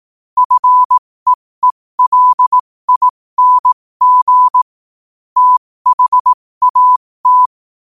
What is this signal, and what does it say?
Morse code, heard as a keyed tone, about 18 wpm, spaced as for 15 wpm, 1000 Hz. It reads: FEELING THAT
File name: qotdfiles / 15wpm